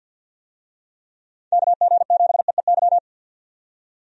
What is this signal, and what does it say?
Morse code, heard as a keyed tone, about 49 wpm, 700 Hz. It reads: KG7EEY
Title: USA Call Signs - 6 Characters - 50wpm